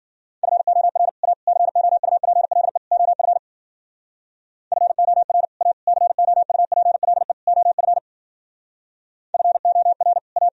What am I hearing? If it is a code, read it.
Morse code, heard as a keyed tone, about 52 wpm, 700 Hz. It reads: FOR A COUPLE OF FOR A COUPLE OF FOR A